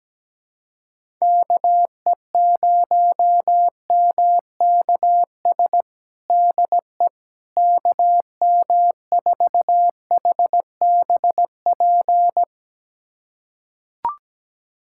Morse code, heard as a keyed tone, about 17 wpm, 700 Hz. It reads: KE0MKS DE KM4HBP
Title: USA Callsign DE Callsign - 6 Characters - 17wpm